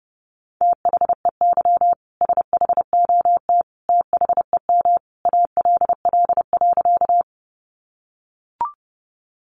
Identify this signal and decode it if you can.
Morse code, heard as a keyed tone, about 30 wpm, 700 Hz. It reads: THEY SHOT THEM ALL.
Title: Sentences from Top 1100 Words 30wpm